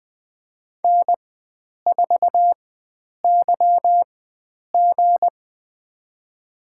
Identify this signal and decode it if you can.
Morse code, heard as a keyed tone, about 20 wpm, spaced as for 9 wpm, 700 Hz. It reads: N4YG
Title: USA Call Signs - 4x Character Spacing - 20wpm